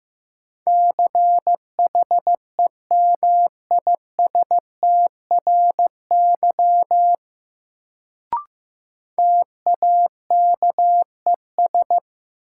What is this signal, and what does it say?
Morse code, heard as a keyed tone, about 15 wpm, 700 Hz. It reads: CHEMISTRY TAKES
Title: Sentences from Top 1000 Words - Review Part 6 - New Word Drill - 15wpm